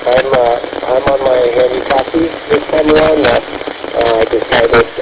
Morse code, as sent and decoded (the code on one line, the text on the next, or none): none